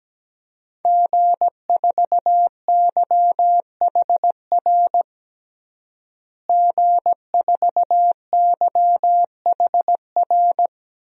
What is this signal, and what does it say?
Morse code, heard as a keyed tone, about 17 wpm, 700 Hz. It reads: G4YHR G4YHR
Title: DXCC-100 Call Signs - 5 Characters - Repeated 3x - 17wpm